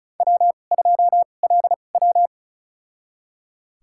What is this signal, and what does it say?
Morse code, heard as a keyed tone, about 36 wpm, spaced as for 25 wpm, 700 Hz. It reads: W2LW
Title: USA Call Signs - 4 Characters - 2x Character Spacing - 35wpm